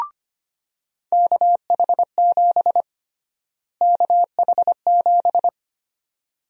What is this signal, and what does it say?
Morse code, heard as a keyed tone, about 25 wpm, 700 Hz. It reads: KH7 KH7
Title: USA Call Sign Prefixes - Repeated 2x - 25wpm